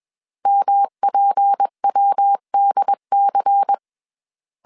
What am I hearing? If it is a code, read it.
Morse code, heard as a keyed tone, about 21 wpm, 800 Hz. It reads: MPWDC
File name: CW / DET / 1 6 1500C S C 20wpm 2116